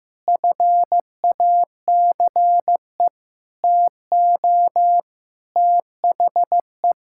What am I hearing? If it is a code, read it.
Morse code, heard as a keyed tone, about 16 wpm, 700 Hz. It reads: FACE TO THE
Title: Sentences from Top 1000 Words - Review Part 6 - 15wpm